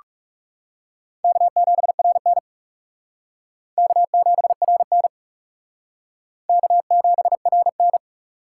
Morse code, heard as a keyed tone, about 35 wpm, 700 Hz. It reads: K7RN K7RN K7RN
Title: Speed Racing - USA 1x2 Call Signs 30wpm